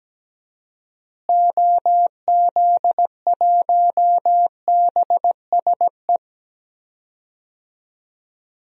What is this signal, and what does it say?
Morse code, heard as a keyed tone, about 17 wpm, 700 Hz. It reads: OZ1BSE
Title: DXCC-100 Call Signs - 6 Characters - Repeated 3x - 17wpm